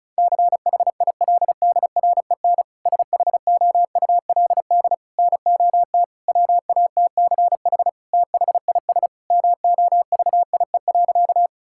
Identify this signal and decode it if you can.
Morse code, heard as a keyed tone, about 34 wpm, 700 Hz. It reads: CHILDREN SHOULD NOT WATCH THIS MOVIE.
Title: Sentences from Top 1000 Words - Review Part 5 - 35wpm